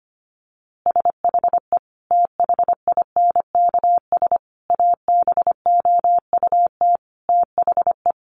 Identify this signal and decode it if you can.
Morse code, heard as a keyed tone, about 25 wpm, 700 Hz. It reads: SHE THINKS ABOUT THE
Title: Easy Phrases 25wpm